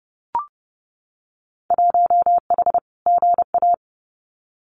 Morse code, heard as a keyed tone, about 30 wpm, 700 Hz. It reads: 1H GA
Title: Winter Field Day Exchanges 30wpm